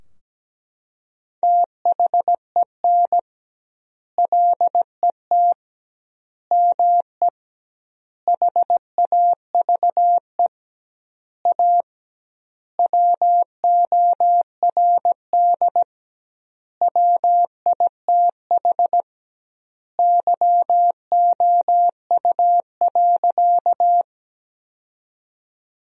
Morse code, heard as a keyed tone, about 17 wpm, 700 Hz. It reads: THEN LET ME HAVE A WORD WITH YOU.
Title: 2x Word-Spacing - Sentences from Top 300 Words - 17wpm